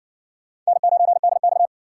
Morse code, heard as a keyed tone, about 60 wpm, 700 Hz. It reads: N9DX